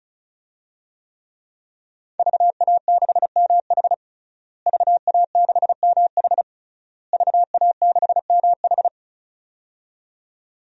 Morse code, heard as a keyed tone, about 34 wpm, 700 Hz. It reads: VA6MH VA6MH VA6MH
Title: DXCC-100 Call Signs - 5 Characters - Repeated 3x - 35wpm